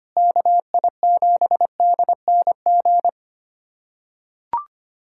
Morse code, heard as a keyed tone, about 25 wpm, 700 Hz. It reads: KI7DNG
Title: USA Callsign DE Callsign - 6 Characters - 25wpm